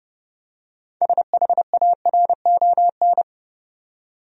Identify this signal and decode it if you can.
Morse code, heard as a keyed tone, about 32 wpm, 700 Hz. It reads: SHARON
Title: USA Common Names - 30wpm